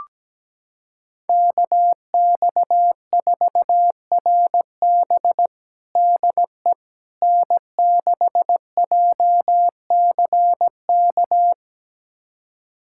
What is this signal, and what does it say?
Morse code, heard as a keyed tone, about 17 wpm, 700 Hz. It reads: KX4RB DE N6JCK
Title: USA Callsign DE Callsign - 5 Characters - 17wpm